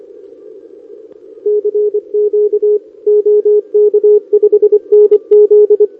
Morse code, heard as a keyed tone, about 27 wpm, 410 Hz. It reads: CQ OK5NZ
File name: OK QRP 2026 / OKQRP2026